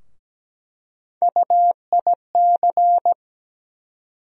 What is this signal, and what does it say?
Morse code, heard as a keyed tone, about 17 wpm, 700 Hz. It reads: UIC
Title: USA Call Sign Postfixes - 3 Letters 17wpm